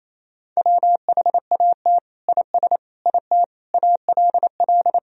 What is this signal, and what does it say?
Morse code, heard as a keyed tone, about 28 wpm, 700 Hz. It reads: WHAT IS IT ALL
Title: Sentences from Top 100 Words and Abbreviations 28wpm